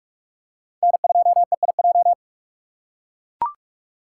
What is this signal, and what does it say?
Morse code, heard as a keyed tone, about 42 wpm, 700 Hz. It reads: N2EIJ
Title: USA Call Signs - 5 Characters - 6x Character Spacing - 45wpm